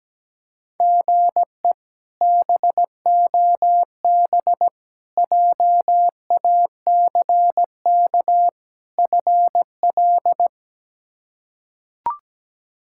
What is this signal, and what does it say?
Morse code, heard as a keyed tone, about 17 wpm, 700 Hz. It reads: GE BOB JACK FL
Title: K1USN SST QSO Practice - 17wpm